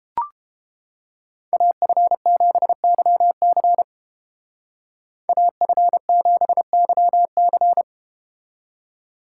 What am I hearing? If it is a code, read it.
Morse code, heard as a keyed tone, about 31 wpm, 700 Hz. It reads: AF7YC AF7YC